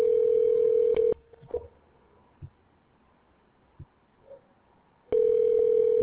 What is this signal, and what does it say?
Ringback tone (North America), 440 + 480 Hz.